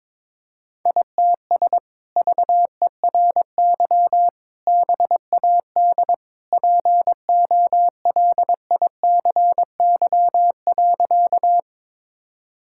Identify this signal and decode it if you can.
Morse code, heard as a keyed tone, about 22 wpm, 700 Hz. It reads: ITS VERY BAD POLICY.
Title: Sentences from Top 1000 Words - Review Part 5 - 22wpm